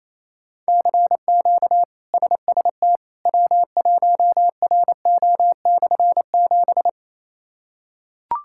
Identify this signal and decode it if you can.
Morse code, heard as a keyed tone, about 28 wpm, 700 Hz. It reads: CQ SST W1RO/7
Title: K1USN SST QSO Practice - 28wpm